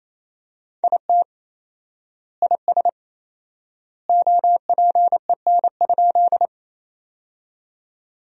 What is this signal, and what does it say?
Morse code, heard as a keyed tone, about 28 wpm, 700 Hz. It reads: IT IS OPEN?
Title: 4x Word-Spacing - Sentences from Top 300 Words - 28wpm